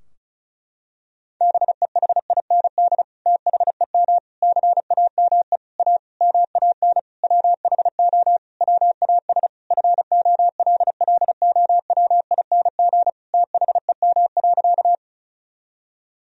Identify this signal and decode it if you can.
Morse code, heard as a keyed tone, about 36 wpm, 700 Hz. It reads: BEHIND THEM CAME A MAN WHO WAS FOLLOWING THEM.
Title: Sentences from Top 1000 Words - Review Part 4 - 35wpm